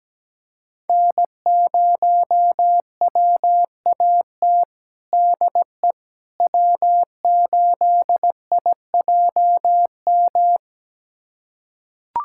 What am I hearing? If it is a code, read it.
Morse code, heard as a keyed tone, about 17 wpm, 700 Hz. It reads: N0WAT DE W8IJM
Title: USA Callsign DE Callsign - 5 Characters - 17wpm